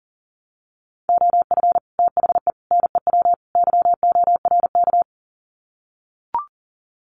Morse code, heard as a keyed tone, about 40 wpm, 700 Hz. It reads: OF THE NEW YORK
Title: Sets of 4 Words 40wpm